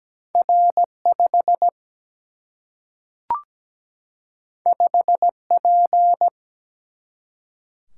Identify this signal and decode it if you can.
Morse code, heard as a keyed tone, about 17 wpm, 700 Hz. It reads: R 5 5 P
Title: Single-Letter Single-Number 17wpm